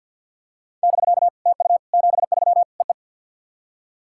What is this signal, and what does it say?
Morse code, heard as a keyed tone, about 48 wpm, 690 Hz. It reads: <BK> TU 73 EE
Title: SOTA QSO Practice 50wpm